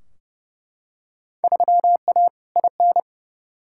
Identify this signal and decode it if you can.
Morse code, heard as a keyed tone, about 30 wpm, 700 Hz. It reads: 3A IN
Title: Field Day Exchanges - 30wpm